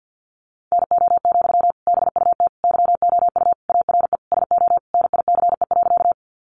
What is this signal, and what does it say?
Morse code, heard as a keyed tone, about 48 wpm, 700 Hz. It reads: NO, BUT YOU ARE SO NICE.
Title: Sentences from Top 1100 Words 50wpm